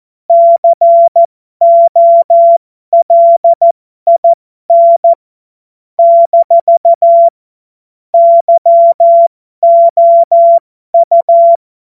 Morse code, heard as a keyed tone, about 14 wpm, spaced as for 12 wpm, 680 Hz. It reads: COLIN - YOU
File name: Morsekurs / Witze